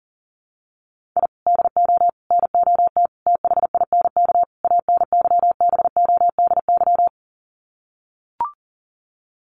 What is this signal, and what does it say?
Morse code, heard as a keyed tone, about 40 wpm, 700 Hz. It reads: I DO NOT THINK ANYBODY
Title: Sets of 5 Words 40wpm